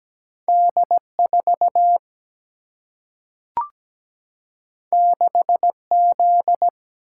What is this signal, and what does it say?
Morse code, heard as a keyed tone, about 17 wpm, 700 Hz. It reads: D4 6Z
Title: Single-Letter Single-Number 17wpm